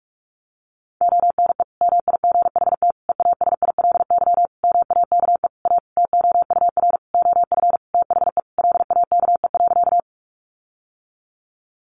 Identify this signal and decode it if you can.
Morse code, heard as a keyed tone, about 44 wpm, 700 Hz. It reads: ONE MIGHT EASILY MAKE A TOUR OF THE LAKE.
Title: Sentences from Top 1400 Words 45wpm